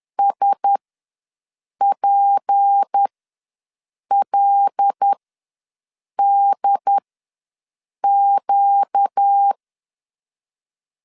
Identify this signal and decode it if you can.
Morse code, heard as a keyed tone, about 10 wpm, spaced as for 6 wpm, 800 Hz. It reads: SPLDQ
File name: LEZIONI / 6 PKX0 / 6 PK ws 10wpm 3745